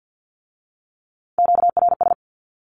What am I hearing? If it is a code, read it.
Morse code, heard as a keyed tone, about 50 wpm, 700 Hz. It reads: QRS
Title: Q-Codes with Spoken Meaning - 50wpm